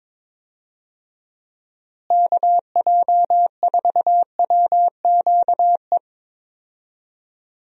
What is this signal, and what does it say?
Morse code, heard as a keyed tone, about 22 wpm, 700 Hz. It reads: KJ4WQE
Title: USA Call Signs - 6 Characters - Repeated 2x - 22wpm